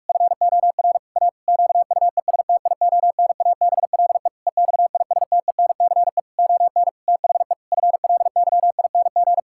Morse code, heard as a keyed tone, about 43 wpm, 700 Hz. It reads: FOR A QUESTIONABLE EXISTENCE ON THE FLYING
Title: Pangram Challenge v2 - 45wpm